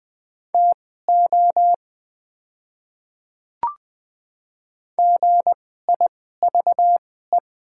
Morse code, heard as a keyed tone, about 20 wpm, spaced as for 15 wpm, 700 Hz. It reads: TO GIVE
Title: 2x Character Spacing - Top 200 Words - 20wpm